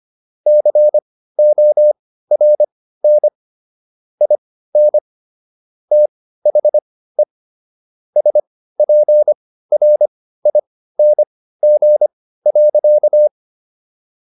Morse code, heard as a keyed tone, about 25 wpm, spaced as for 15 wpm, 600 Hz. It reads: CORN IN THE SPRING.